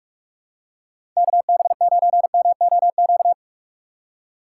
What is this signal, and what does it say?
Morse code, heard as a keyed tone, about 46 wpm, 700 Hz. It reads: KB9MOQ
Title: USA Call Signs - 6 Characters - 2x Character Spacing - 45wpm